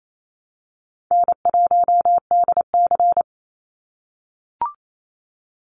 Morse code, heard as a keyed tone, about 28 wpm, 700 Hz. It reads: N1DC
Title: Call Signs - 28wpm